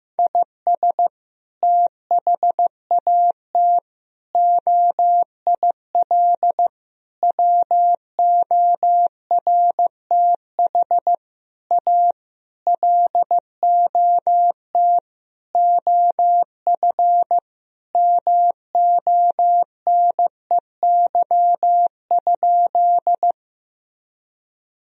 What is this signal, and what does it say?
Morse code, heard as a keyed tone, about 15 wpm, 700 Hz. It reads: IS THAT OIL WORTH A LOT OF MONEY?
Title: Sentences from Top 1000 Words - Review Part 4 - 15wpm